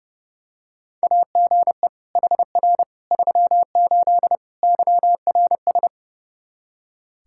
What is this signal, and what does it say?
Morse code, heard as a keyed tone, about 29 wpm, 700 Hz. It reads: AGE HR 38 YRS
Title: Practice QSO - Exchange 3 - More Info - 30wpm